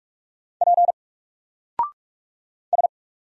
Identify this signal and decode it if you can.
Morse code, heard as a keyed tone, about 44 wpm, 700 Hz. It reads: P S
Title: Single Letter-Number Alt 45wpm